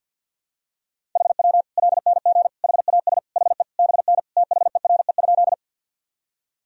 Morse code, heard as a keyed tone, about 52 wpm, 700 Hz. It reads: HW LNG HAS HE BN THERE?
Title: Sentences from Top 100 Words and Abbreviations 50wpm